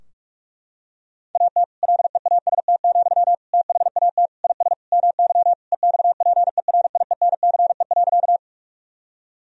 Morse code, heard as a keyed tone, about 42 wpm, 700 Hz. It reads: AT LEAST, THAT IS MY EXPERIENCE.
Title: Sentences from Top 1000 Words - Review Part 3 - 45wpm